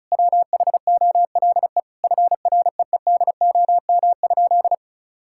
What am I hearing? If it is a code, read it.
Morse code, heard as a keyed tone, about 36 wpm, 700 Hz. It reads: WHOLE FREEDOM?
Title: Sentences from Top 1000 Words - Review Part 5 - 35wpm